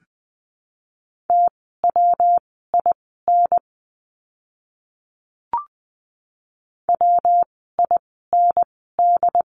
Morse code, heard as a keyed tone, about 20 wpm, spaced as for 14 wpm, 700 Hz. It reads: TWIN WIND
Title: 115 Lesson 19f - Letter w - Words 20wpm